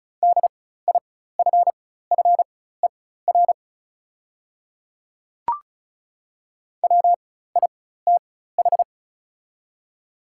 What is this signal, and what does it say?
Morse code, heard as a keyed tone, about 36 wpm, spaced as for 16 wpm, 690 Hz. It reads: DIFFER WITH
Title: 4x Character Spacing - Top 200 Words - 35wpm